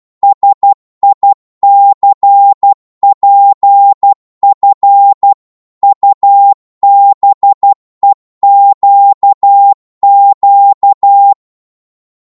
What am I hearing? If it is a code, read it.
Morse code, heard as a keyed tone, about 12 wpm, 800 Hz. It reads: SICPF UBEQQ
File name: Morse / ltr-12wpm